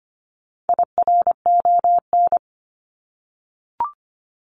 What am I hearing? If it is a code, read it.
Morse code, heard as a keyed tone, about 25 wpm, 700 Hz. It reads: IRON
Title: Top 1751-2000 Words 25wpm